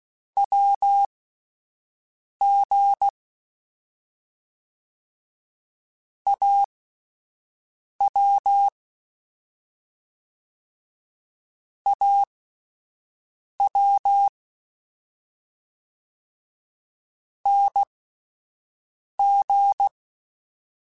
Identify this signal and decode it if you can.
Morse code, heard as a keyed tone, about 16 wpm, spaced as for 5 wpm, 800 Hz. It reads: WG AW AW NG